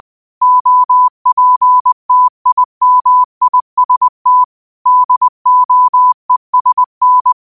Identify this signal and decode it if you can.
Morse code, heard as a keyed tone, about 20 wpm, 1000 Hz. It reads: OPTIMIST DOESN
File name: qotdfiles / 20wpm